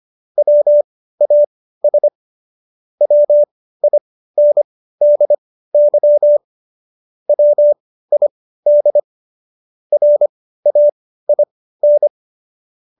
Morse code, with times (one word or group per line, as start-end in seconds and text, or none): 0.4-2.1 WAS
3.0-6.4 WINDY
7.3-9.0 WID
9.9-12.1 RAIN